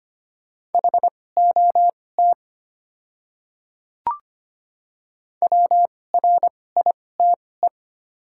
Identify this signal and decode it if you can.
Morse code, heard as a keyed tone, about 25 wpm, spaced as for 18 wpm, 700 Hz. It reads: HOT WRITE
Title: 2x Character Spacing - Top 200 Words - 25wpm